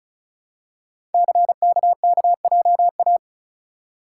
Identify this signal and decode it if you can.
Morse code, heard as a keyed tone, about 36 wpm, 700 Hz. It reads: CKKJA
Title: Code Groups - 5 Letters - 35wpm